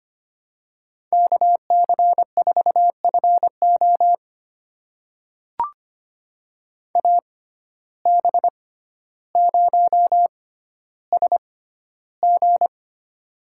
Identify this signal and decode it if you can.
Morse code, heard as a keyed tone, about 25 wpm, spaced as for 13 wpm, 700 Hz. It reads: KC4FO AB0SG
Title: USA Call Signs - 5 Characters - 6x Character Spacing - 25wpm